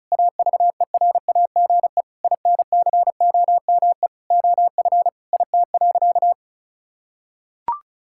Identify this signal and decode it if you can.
Morse code, heard as a keyed tone, about 36 wpm, 700 Hz. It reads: AVERAGE INCOME OF IT.
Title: Sentences from Top 1000 Words - Review Part 6 - 35wpm